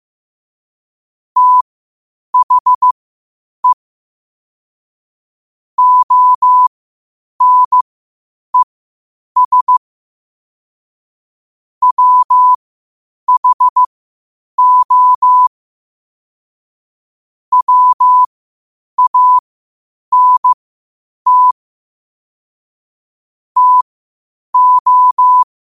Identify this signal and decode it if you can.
Morse code, heard as a keyed tone, about 15 wpm, spaced as for 8 wpm, 1000 Hz. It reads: THE ONES WHO WANT TO
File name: qotdfiles / 8wpm